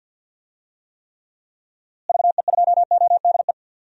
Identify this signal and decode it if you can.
Morse code, heard as a keyed tone, about 46 wpm, 700 Hz. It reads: VE2ODE
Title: DXCC-100 Call Signs - 6 Characters - Repeated 2x - 50wpm